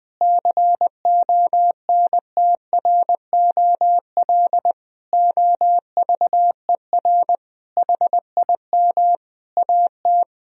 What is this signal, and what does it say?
Morse code, heard as a keyed tone, about 20 wpm, 700 Hz. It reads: CONTROL OVER HIM AT
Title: Sentences from Top 1000 Words - Review Part 3 - 20wpm